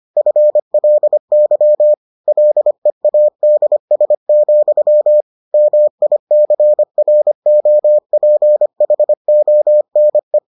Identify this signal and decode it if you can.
Morse code, heard as a keyed tone, about 25 wpm, 600 Hz. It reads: FLY LEADS, MICROPHONE